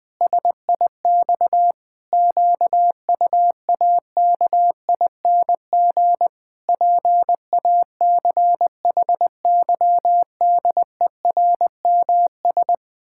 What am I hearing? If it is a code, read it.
Morse code, heard as a keyed tone, about 20 wpm, 700 Hz. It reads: SIX QUAKING PACHYDERMS
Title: Pangram Challenge v2 - 20wpm